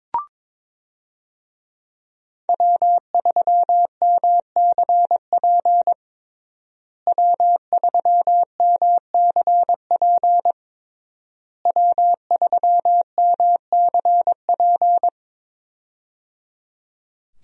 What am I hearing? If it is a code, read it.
Morse code, heard as a keyed tone, about 22 wpm, 700 Hz. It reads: W3MCP W3MCP W3MCP
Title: USA Call Signs - 5 Characters - Repeated 3x - 22wpm